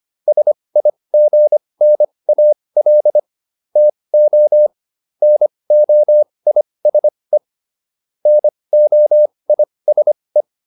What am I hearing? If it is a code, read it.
Morse code, heard as a keyed tone, about 25 wpm, spaced as for 20 wpm, 600 Hz. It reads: SIGNAL TO NOISE NOISE